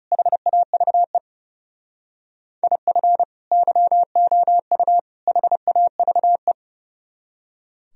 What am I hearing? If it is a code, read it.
Morse code, heard as a keyed tone, about 38 wpm, 700 Hz. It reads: HAVE IF YOU HAVE
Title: Speed-Racing - Sets of 3 Words 30wpm